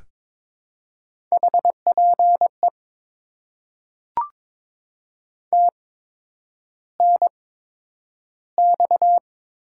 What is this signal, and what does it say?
Morse code, heard as a keyed tone, about 22 wpm, spaced as for 9 wpm, 700 Hz. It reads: HPE TNX
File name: Top 90 QSO Elements - 8x Character Spacing - 22wpm